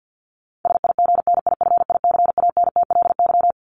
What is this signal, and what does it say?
Morse code, heard as a keyed tone, about 50 wpm, 700 Hz. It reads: SIGNIFICANTLY
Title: LY Suffix - Top 100 Words with Base Words 50wpm